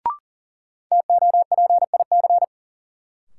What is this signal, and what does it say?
Morse code, heard as a keyed tone, about 40 wpm, 700 Hz. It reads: TOPIC
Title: Sentences from Top 1200 Words - New Word Drill Alt - 40wpm